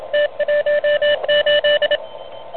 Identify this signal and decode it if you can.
Morse code, heard as a keyed tone, about 26 wpm, 600 Hz. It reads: T18
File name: truSDX / NZ contest